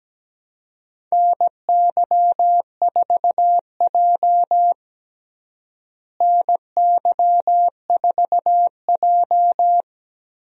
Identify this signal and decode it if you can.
Morse code, heard as a keyed tone, about 17 wpm, 700 Hz. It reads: NY4J NY4J
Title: USA Call Signs - 4 Characters - Repeated 3x - 17wpm